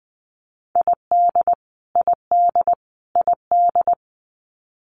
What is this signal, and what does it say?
Morse code, heard as a keyed tone, about 20 wpm, 700 Hz. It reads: ID ID ID
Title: US States Repeated 3x 20wpm